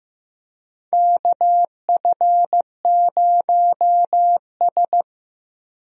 Morse code, heard as a keyed tone, about 15 wpm, 700 Hz. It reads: KF0S
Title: USA Call Signs - 4 Characters - Repeated 3x - 15wpm